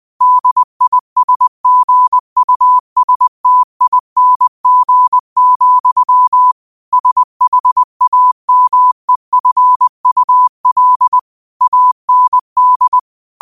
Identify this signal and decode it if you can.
Morse code, heard as a keyed tone, about 20 wpm, 1000 Hz. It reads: DISGUSTING, SHAMEFUL AND
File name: qotdfiles / 20wpm